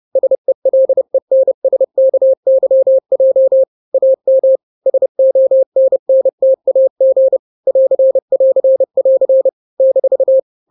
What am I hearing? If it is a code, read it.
Morse code, heard as a keyed tone, about 30 wpm, 530 Hz. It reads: SELENSKYJ AM SONNTAG <AR><AR><AR> -
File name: 29wpm